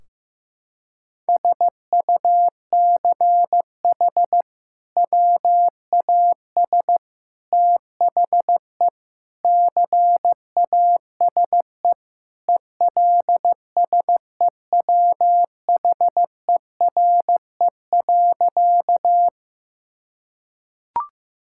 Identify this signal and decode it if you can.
Morse code, heard as a keyed tone, about 15 wpm, 700 Hz. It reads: SUCH WAS THE CASE ELSEWHERE.
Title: Sentences from Top 1600 Words 15wpm